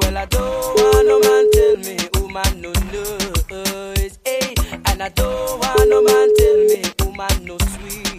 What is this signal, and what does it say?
Ringback tone (Europe, 425 Hz), with noise.